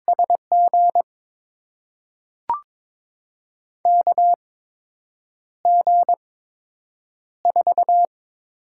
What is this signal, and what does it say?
Morse code, heard as a keyed tone, about 22 wpm, spaced as for 8 wpm, 700 Hz. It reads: SG KG4
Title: USA Call Signs - 6 Characters - 8x Character Spacing - 22wpm